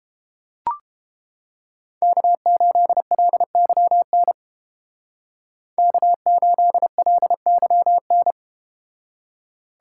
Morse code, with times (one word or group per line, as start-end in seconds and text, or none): 2.0-4.3 K8LYN
5.8-8.3 K8LYN